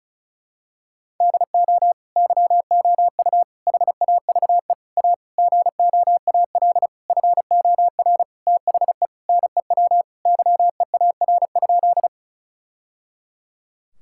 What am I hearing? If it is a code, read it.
Morse code, heard as a keyed tone, about 36 wpm, 700 Hz. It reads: DO YOU HAVE A GOAL FOR THE NEW YEAR?
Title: Sentences from Top 1000 Words - Review Part 5 - 35wpm